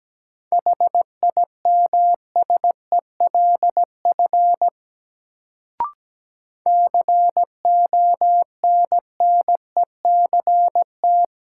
Morse code, heard as a keyed tone, about 17 wpm, 700 Hz. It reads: HIMSELF CONNECT
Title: Sentences from Top 1500 Words - New Word Drill - 17wpm